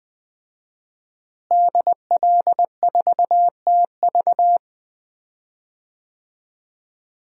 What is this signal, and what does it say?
Morse code, heard as a keyed tone, about 20 wpm, 700 Hz. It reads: DL4TV
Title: DXCC-100 Call Signs - 5 Characters - Repeated 3x - 20wpm